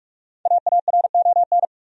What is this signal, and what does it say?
Morse code, heard as a keyed tone, about 46 wpm, 700 Hz. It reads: AARON
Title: USA Common Names - 45wpm